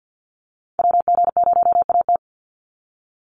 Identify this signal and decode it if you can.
Morse code, heard as a keyed tone, about 48 wpm, 700 Hz. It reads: WG0AT
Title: NA SOTA Chaser Callsigns 50wpm